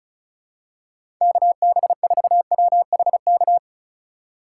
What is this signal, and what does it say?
Morse code, heard as a keyed tone, about 35 wpm, 700 Hz. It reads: KB4WHK
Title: USA Call Signs - 6 Characters - 3x Character Spacing - 35wpm